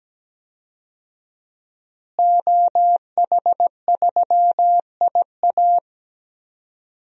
Morse code, heard as a keyed tone, about 17 wpm, 700 Hz. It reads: OH3IA
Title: DXCC-100 Call Signs - 5 Characters - Repeated 3x - 17wpm